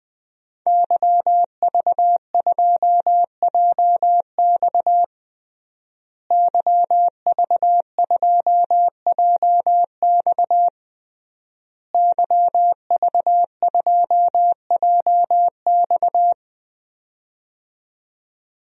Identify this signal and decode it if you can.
Morse code, heard as a keyed tone, about 20 wpm, 700 Hz. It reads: YV2JX YV2JX YV2JX
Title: DXCC-100 Call Signs - 5 Characters - Repeated 3x - 20wpm